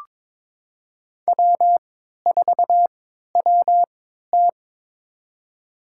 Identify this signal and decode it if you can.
Morse code, heard as a keyed tone, about 22 wpm, spaced as for 12 wpm, 700 Hz. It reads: W4WT
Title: USA Call Signs - 3x Character Spacing - 22wpm